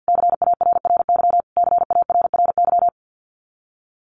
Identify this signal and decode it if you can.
Morse code, heard as a keyed tone, about 50 wpm, 700 Hz. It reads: CARRY CARRY
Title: Top 500 Words Repeated 50wpm